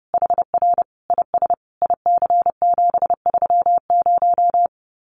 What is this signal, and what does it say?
Morse code, heard as a keyed tone, about 28 wpm, 700 Hz. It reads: HR IS IC730
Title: Common QSO Phrases 30wpm